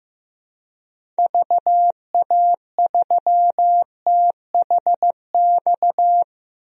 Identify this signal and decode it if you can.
Morse code, heard as a keyed tone, about 15 wpm, 700 Hz. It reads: VA3THX
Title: DXCC-100 Call Signs - 6 Characters - 15wpm